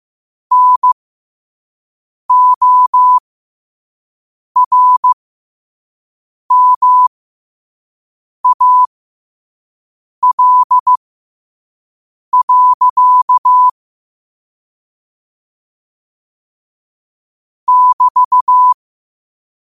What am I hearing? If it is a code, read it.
Morse code, heard as a keyed tone, about 15 wpm, spaced as for 5 wpm, 1000 Hz. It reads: NORMAL. <BT>
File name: qotdfiles / 5wpm